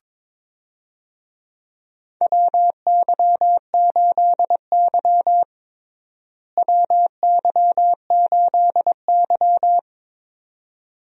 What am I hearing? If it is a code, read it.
Morse code, heard as a keyed tone, about 22 wpm, 700 Hz. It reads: WY8Y WY8Y
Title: USA Call Signs - 4 Characters - Repeated 2x - 22wpm